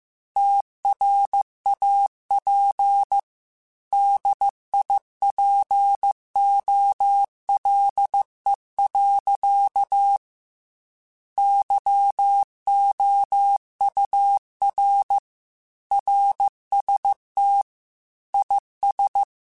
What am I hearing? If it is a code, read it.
Morse code, heard as a keyed tone, about 15 wpm, 800 Hz. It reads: TRAP DIPOLE. YOUR RST IS
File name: QSOs / QSOs-mp3s / WD0FBT 1@13